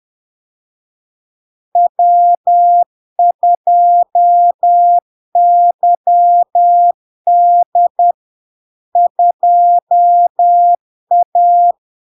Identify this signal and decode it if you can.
Morse code, heard as a keyed tone, about 10 wpm, 700 Hz. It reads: W2YD 2A
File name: LARK-FD-2022-Log-in-CW-10wpm-700-Hz